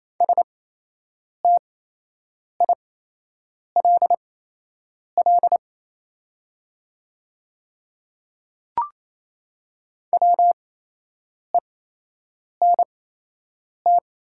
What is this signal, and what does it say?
Morse code, heard as a keyed tone, about 28 wpm, spaced as for 8 wpm, 700 Hz. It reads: STILL WENT
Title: 8x Character Spacing - Top 500 Words - 28wpm